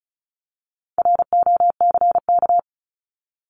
Morse code, heard as a keyed tone, about 34 wpm, 700 Hz. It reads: ROCK
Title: Top 500 Words Repeated 35wpm